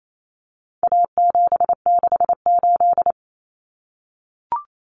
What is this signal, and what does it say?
Morse code, heard as a keyed tone, about 28 wpm, 700 Hz. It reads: A768
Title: Common Cut Numbers - 4-Digit 28wpm